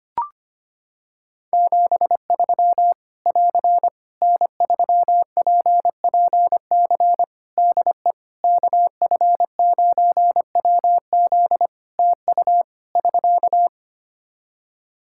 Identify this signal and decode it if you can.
Morse code, heard as a keyed tone, about 25 wpm, 700 Hz. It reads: 73 <AR> N3PPC DE KF9WZ TU <SK>
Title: K1USN SST QSO Practice 25wpm